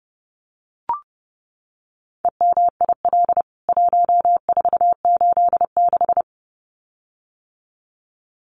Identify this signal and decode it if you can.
Morse code, heard as a keyed tone, about 30 wpm, 700 Hz. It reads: EMIL 1486
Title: CWT Exchange 30wpm